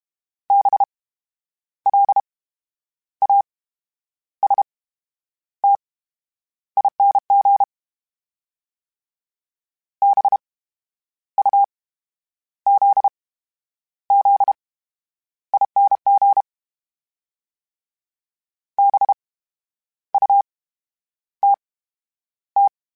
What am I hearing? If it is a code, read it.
Morse code, heard as a keyed tone, about 32 wpm, spaced as for 10 wpm, 800 Hz. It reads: BLASTING BUZZING BUTT
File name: Special INGs / Special INGs-mp3s / VerySlowToFaster Ing @30